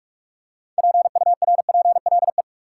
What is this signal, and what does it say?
Morse code, heard as a keyed tone, about 46 wpm, 700 Hz. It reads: PURPLE
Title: Six Letter Words - 45wpm